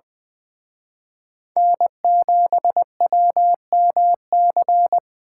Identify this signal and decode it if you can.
Morse code, heard as a keyed tone, about 25 wpm, 700 Hz. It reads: N7WMC
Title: Speed Racing - USA 1x3 Call Signs 20wpm